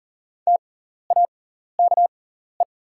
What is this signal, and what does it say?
Morse code, heard as a keyed tone, about 40 wpm, spaced as for 14 wpm, 690 Hz. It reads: TAKE
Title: 6x Character Spacing - Top 200 Words - 40wpm